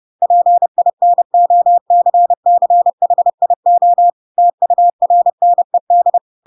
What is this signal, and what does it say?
Morse code, heard as a keyed tone, about 30 wpm, 700 Hz. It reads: PINOCCHIO TURNED
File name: Pinocchio / Pinocchio ch4 30 30 ik1hge